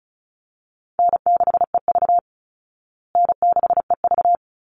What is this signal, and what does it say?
Morse code, heard as a keyed tone, about 34 wpm, 700 Hz. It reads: N6EV N6EV
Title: Call Signs Repeated 3x - 35wpm